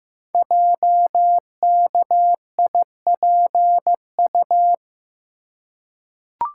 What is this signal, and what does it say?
Morse code, heard as a keyed tone, about 15 wpm, 700 Hz. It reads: JKIPU
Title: Code Groups - 5 Letters - 15wpm